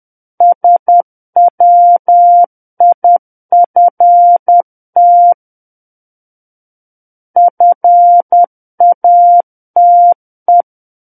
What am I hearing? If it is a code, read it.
Morse code, heard as a keyed tone, about 10 wpm, 700 Hz. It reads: SWIFT FATE
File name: Book One / 01-Chapter 1 - 10 WPM